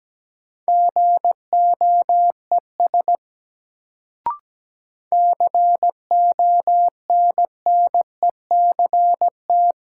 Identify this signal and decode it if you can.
Morse code, heard as a keyed tone, about 17 wpm, 700 Hz. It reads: GOES CONNECT
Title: Sentences from Top 1500 Words - New Word Drill Alt - 17wpm